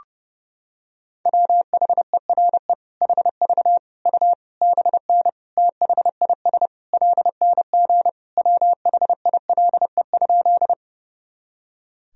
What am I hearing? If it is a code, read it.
Morse code, heard as a keyed tone, about 32 wpm, 700 Hz. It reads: WHERE HV U BN THIS LNG WHILE?
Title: Sentences from Top 300 Words and Abbreviations 30wpm